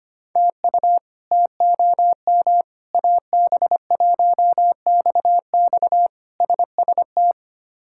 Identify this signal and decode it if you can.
Morse code, heard as a keyed tone, about 25 wpm, 700 Hz. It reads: TU TOM AB1XX SST
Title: K1USN SST QSO Practice - 25wpm